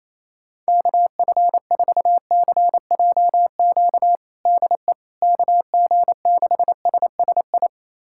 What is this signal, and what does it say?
Morse code, heard as a keyed tone, about 27 wpm, 700 Hz. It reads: KF4CJQ DE KG6SSI
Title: USA Callsign DE Callsign - 6 Characters - 28wpm